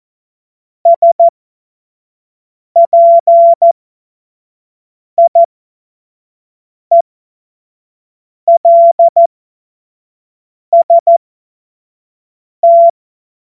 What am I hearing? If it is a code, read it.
Morse code, heard as a keyed tone, about 14 wpm, spaced as for 5 wpm, 680 Hz. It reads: SPIELST
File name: Morsekurs / Witze / Witz 08. Februar 2022 25 F70